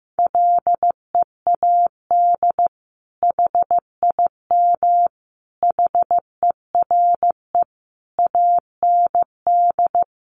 Morse code, heard as a keyed tone, about 15 wpm, 700 Hz. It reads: LEAD HIM HERE AND
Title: Sentences from Top 1200 Words 15wpm